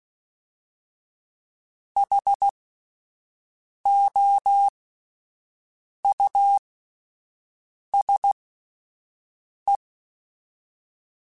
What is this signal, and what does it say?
Morse code, heard as a keyed tone, about 16 wpm, spaced as for 5 wpm, 800 Hz. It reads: HOUSE
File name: General / General-mp3s / Houses @10